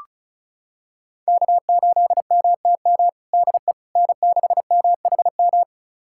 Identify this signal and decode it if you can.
Morse code, heard as a keyed tone, about 34 wpm, 700 Hz. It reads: K8MTM DE N6MHM
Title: USA Callsign DE Callsign - 5 Characters - 35wpm